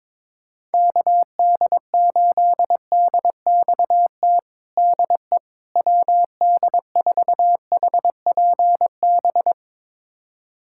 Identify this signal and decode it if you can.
Morse code, heard as a keyed tone, about 22 wpm, 700 Hz. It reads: KD8DXT DE WD4HPB
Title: USA Callsign DE Callsign - 6 Characters - 22wpm